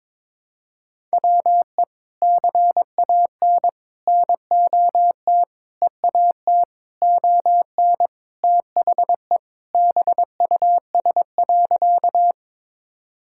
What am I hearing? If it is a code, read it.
Morse code, heard as a keyed tone, about 22 wpm, 700 Hz. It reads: WE CAN NOT EAT ON THE BUS.
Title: Sentences from Top 1000 Words - Review Part 6 - 22wpm